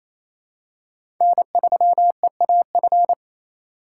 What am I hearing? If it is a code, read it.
Morse code, heard as a keyed tone, about 27 wpm, 700 Hz. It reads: N3EAF
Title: USA Call Signs - 5 Characters - 4x Character Spacing - 28wpm